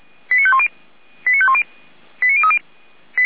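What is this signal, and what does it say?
Five-tone selective call (ZVEI), heard as quick runs of five tones, tones of about 70 ms. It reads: 89510 89510 89030
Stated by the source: ZVEI five-tone selective call received on SDR.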